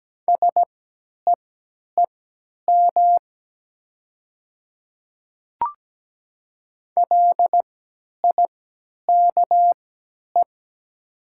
Morse code, heard as a keyed tone, about 17 wpm, spaced as for 10 wpm, 700 Hz. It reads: SEEM LIKE
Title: 3x Character Spacing - Top 300 Words - 17wpm